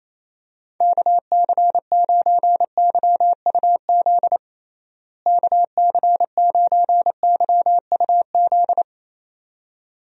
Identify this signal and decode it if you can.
Morse code, heard as a keyed tone, about 28 wpm, 700 Hz. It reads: KC9YUZ KC9YUZ
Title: USA Call Signs - 6 Characters - Repeated 2x - 28wpm